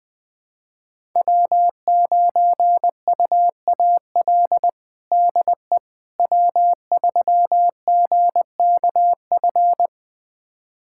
Morse code, heard as a keyed tone, about 20 wpm, 700 Hz. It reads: W9UAL DE W3GKF
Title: USA Callsign DE Callsign - 5 Characters - 20wpm